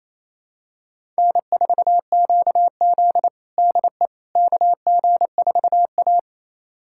Morse code, heard as a keyed tone, about 28 wpm, 700 Hz. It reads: N4QZ DE KG4A
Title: USA Callsign DE Callsign - 4 Characters - 28wpm